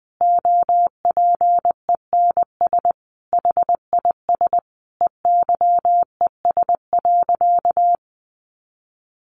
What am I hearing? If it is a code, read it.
Morse code, heard as a keyed tone, about 20 wpm, 700 Hz. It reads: OPENS HIS EYES.
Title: Easy Phrases 20wpm